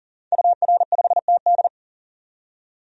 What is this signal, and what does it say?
Morse code, heard as a keyed tone, about 40 wpm, 700 Hz. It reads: UR5TD
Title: DXCC-100 Call Signs - 5 Characters - 40wpm